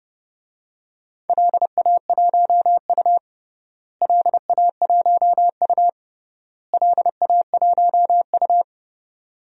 Morse code, heard as a keyed tone, about 32 wpm, 700 Hz. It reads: LA1U LA1U LA1U
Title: DXCC-100 Call Signs - 4 Characters - Repeated 3x - 30wpm